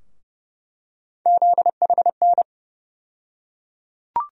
Morse code, heard as a keyed tone, about 30 wpm, 700 Hz. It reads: ZHN
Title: USA Call Sign Postfixes - 3 Letters 30wpm